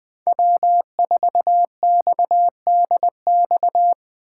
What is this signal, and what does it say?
Morse code, heard as a keyed tone, about 24 wpm, 700 Hz. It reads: W4XDX
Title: Speed Racing - USA 1x3 Call Signs 20wpm